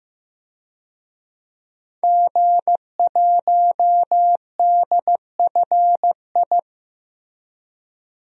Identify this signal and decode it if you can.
Morse code, heard as a keyed tone, about 16 wpm, 700 Hz. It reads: G1DFI
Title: DXCC-100 Call Signs - 5 Characters - Repeated 3x - 15wpm